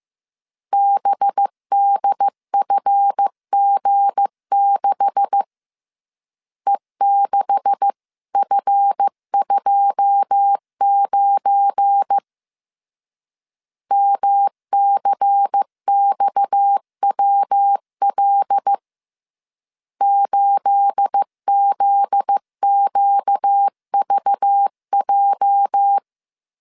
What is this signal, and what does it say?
Morse code, heard as a keyed tone, about 15 wpm, 800 Hz. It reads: BDFG6 E6F29 MCXWL 8ZQVJ
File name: CW / DET / 1 9 1500C S A 14wpm 3222